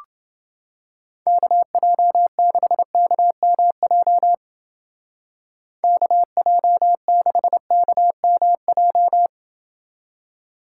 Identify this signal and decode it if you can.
Morse code, heard as a keyed tone, about 29 wpm, 700 Hz. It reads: KJ6KMJ KJ6KMJ